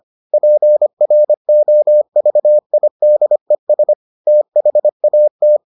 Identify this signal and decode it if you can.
Morse code, heard as a keyed tone, about 25 wpm, 600 Hz. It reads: PROVIDES THAT